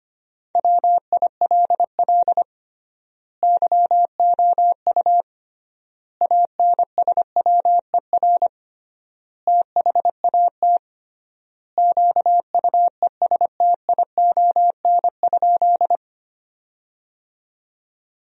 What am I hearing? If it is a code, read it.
Morse code, heard as a keyed tone, about 25 wpm, 700 Hz. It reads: WILL YOU ANSWER THAT QUESTION?
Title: 3x Word-Spacing - Sentences from Top 400 Words - 25wpm